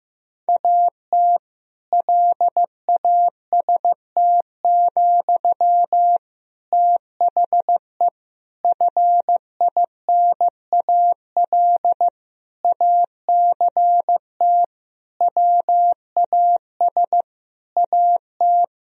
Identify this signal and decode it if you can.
Morse code, heard as a keyed tone, about 16 wpm, 700 Hz. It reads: AT LAST, THE FINAL ACT WAS AT
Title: Sentences from Top 1000 Words - Review Part 2 - 15wpm